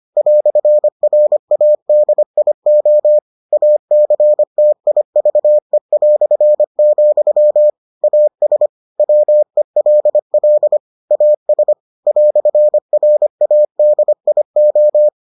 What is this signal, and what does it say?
Morse code, heard as a keyed tone, about 25 wpm, 600 Hz. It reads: "RADIO ACTIVE", AS WELL AS "RADIO